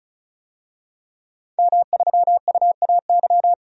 Morse code, heard as a keyed tone, about 34 wpm, 700 Hz. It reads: M3UAY
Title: DXCC-100 Call Signs - 5 Characters - 35wpm